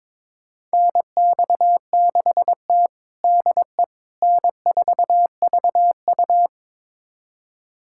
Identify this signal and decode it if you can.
Morse code, heard as a keyed tone, about 22 wpm, 700 Hz. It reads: NX6T DE N4VU
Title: USA Callsign DE Callsign - 4 Characters - 22wpm